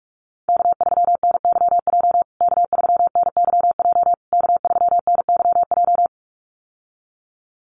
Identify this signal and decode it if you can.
Morse code, heard as a keyed tone, about 44 wpm, 700 Hz. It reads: K3NYJ K3NYJ K3NYJ
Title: USA Call Signs Repeated 3x 45wpm